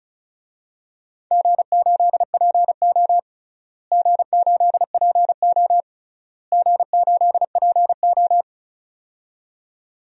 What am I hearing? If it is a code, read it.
Morse code, heard as a keyed tone, about 36 wpm, 700 Hz. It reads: G8PO G8PO G8PO
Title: DXCC-100 Call Signs - 4 Characters - Repeated 3x - 35wpm